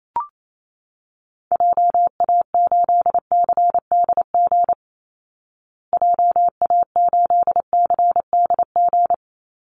Morse code, heard as a keyed tone, about 28 wpm, 700 Hz. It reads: JA8CDG JA8CDG
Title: Call Signs Repeated 2x - 28wpm